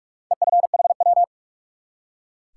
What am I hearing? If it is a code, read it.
Morse code, heard as a keyed tone, about 43 wpm, 700 Hz. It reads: EFHW
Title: Top 180 QSO Elements 45wpm